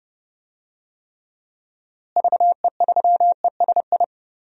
Morse code, heard as a keyed tone, about 29 wpm, 700 Hz. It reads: VE3ESI
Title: DXCC-100 Call Signs - 6 Characters - Repeated 2x - 30wpm